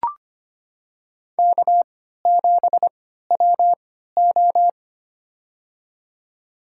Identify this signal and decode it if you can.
Morse code, heard as a keyed tone, about 25 wpm, spaced as for 14 wpm, 700 Hz. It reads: K7WO
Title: USA Call Signs - 3x Character Spacing - 25wpm